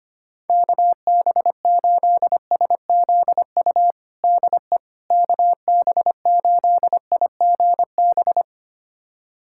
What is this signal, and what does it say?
Morse code, heard as a keyed tone, about 25 wpm, 700 Hz. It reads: KB8SZU DE KB8IGB
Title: USA Callsign DE Callsign - 6 Characters - 25wpm